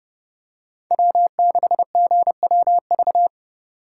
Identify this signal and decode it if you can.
Morse code, heard as a keyed tone, about 28 wpm, 700 Hz. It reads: W6GWV